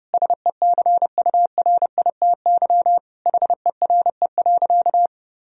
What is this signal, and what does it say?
Morse code, heard as a keyed tone, about 30 wpm, 700 Hz. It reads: SECURITY HERE.
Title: Sentences from Top 1000 Words - Review Part 4 - 30wpm